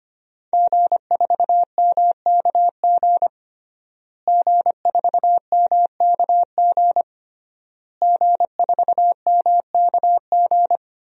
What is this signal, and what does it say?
Morse code, heard as a keyed tone, about 25 wpm, 700 Hz. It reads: G4MKG G4MKG G4MKG
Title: DXCC-100 Call Signs - 5 Characters - Repeated 3x - 25wpm